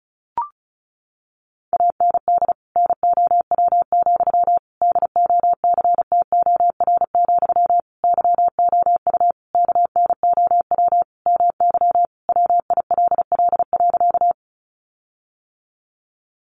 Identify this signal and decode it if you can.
Morse code, heard as a keyed tone, about 34 wpm, 700 Hz. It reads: AND NOW, DOCTOR, YOU KNOW MY WILL.
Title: Sentences from Top 1500 Words 35wpm